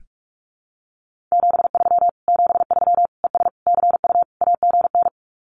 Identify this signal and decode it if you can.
Morse code, heard as a keyed tone, about 44 wpm, 700 Hz. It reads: 73 73 ES CU AGN
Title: Common QSO Phrases Encore 45wpm